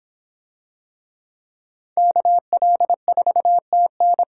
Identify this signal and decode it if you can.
Morse code, heard as a keyed tone, about 26 wpm, 700 Hz. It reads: KL4TN